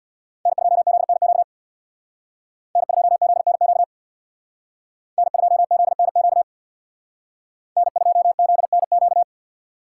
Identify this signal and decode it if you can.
Morse code, heard as a keyed tone, about 62 wpm, 700 Hz. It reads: N2ZNQ N2ZNQ N2ZNQ N2ZNQ
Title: Speed Racing - USA Call Signs 50wpm